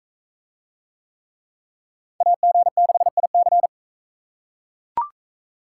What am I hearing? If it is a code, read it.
Morse code, heard as a keyed tone, about 41 wpm, 700 Hz. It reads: AG6IC